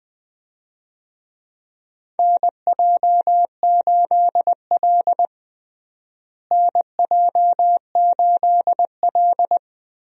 Morse code, heard as a keyed tone, about 20 wpm, 700 Hz. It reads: NJ8L NJ8L
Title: USA Call Signs - 4 Characters - Repeated 3x - 20wpm